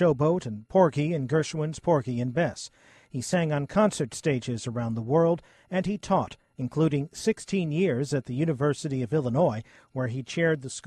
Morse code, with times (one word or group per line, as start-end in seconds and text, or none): none